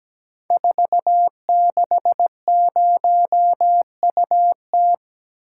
Morse code, heard as a keyed tone, about 17 wpm, 700 Hz. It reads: 460UT
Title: Code Groups - 5 Letter:Numbers - 17wpm